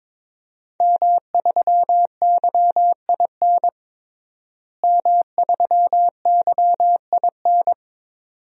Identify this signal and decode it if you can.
Morse code, heard as a keyed tone, about 22 wpm, 700 Hz. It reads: M3YIN M3YIN
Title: DXCC-100 Call Signs - 5 Characters - Repeated 2x - 22wpm